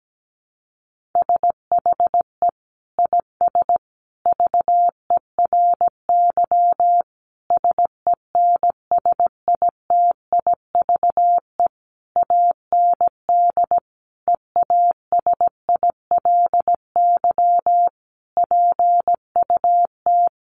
Morse code, heard as a keyed tone, about 17 wpm, 700 Hz. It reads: SHE IS VERY SENSITIVE AND EASILY PUT
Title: Sentences from Top 1400 Words 17wpm